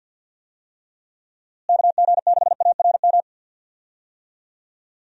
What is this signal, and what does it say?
Morse code, heard as a keyed tone, about 51 wpm, 700 Hz. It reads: KG6ARM
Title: USA Call Signs - 6 Characters - Repeated 2x - 50wpm